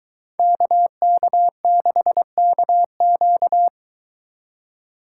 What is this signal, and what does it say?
Morse code, heard as a keyed tone, about 22 wpm, 700 Hz. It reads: KK6KQ